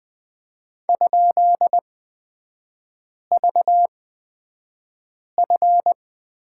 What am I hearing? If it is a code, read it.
Morse code, heard as a keyed tone, about 20 wpm, spaced as for 5 wpm, 690 Hz. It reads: ?VF
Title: 229 Lesson 37e - Letter x - Three Characters 20wpm